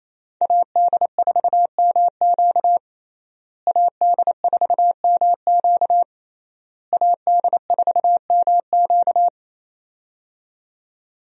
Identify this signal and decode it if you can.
Morse code, heard as a keyed tone, about 28 wpm, 700 Hz. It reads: AD4MQ AD4MQ AD4MQ
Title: USA Call Signs - 5 Characters - Repeated 3x - 28wpm